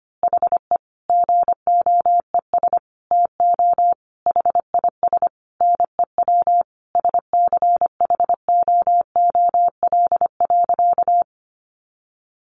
Morse code, heard as a keyed tone, about 25 wpm, 700 Hz. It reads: HE GOES TO HIS NEW SCHOOL.
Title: Easy Phrases 25wpm